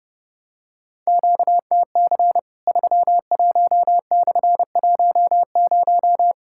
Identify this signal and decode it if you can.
Morse code, heard as a keyed tone, about 30 wpm, 700 Hz. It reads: QTC 31/10
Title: Worked All Europe DX Contest - QTC Exchange - 30wpm